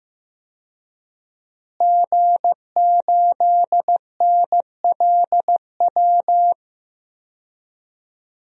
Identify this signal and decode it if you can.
Morse code, heard as a keyed tone, about 15 wpm, 700 Hz. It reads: G8NLW
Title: DXCC-100 Call Signs - 5 Characters - Repeated 2x - 15wpm